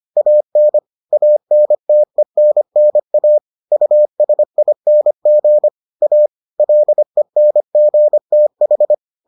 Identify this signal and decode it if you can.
Morse code, heard as a keyed tone, about 25 wpm, 600 Hz. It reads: AN ANTENNA USING A LENGTH